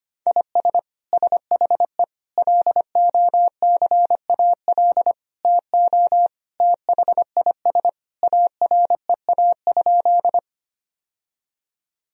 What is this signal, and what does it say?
Morse code, heard as a keyed tone, about 25 wpm, 700 Hz. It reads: IS SHE LOCAL TO THIS AREA?
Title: Sentences from Top 1000 Words - Review Part 3 - 25wpm